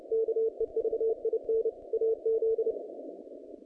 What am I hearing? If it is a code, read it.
Morse code, heard as a keyed tone, about 28 wpm, 450 Hz. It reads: KEVIN AZ
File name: contests / naqp-cw-k5tr-op-k5pi- -0000